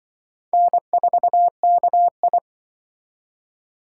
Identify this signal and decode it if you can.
Morse code, heard as a keyed tone, about 25 wpm, 700 Hz. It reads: N4KI
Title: Speed Racing - USA 1x2 Call Signs 20wpm